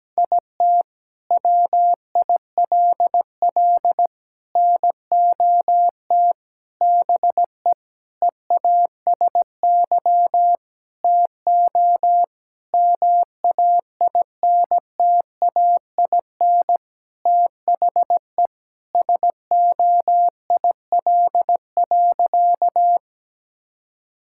Morse code, heard as a keyed tone, about 17 wpm, 700 Hz. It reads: IT WILL NOT BE EASY TO MAINTAIN THE SOIL.
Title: Sentences from Top 1000 Words - Review Part 5 - 17wpm